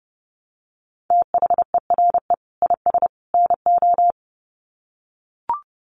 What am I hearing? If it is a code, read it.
Morse code, heard as a keyed tone, about 30 wpm, 700 Hz. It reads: THERE IS NO
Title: Sets of 3 Words 30wpm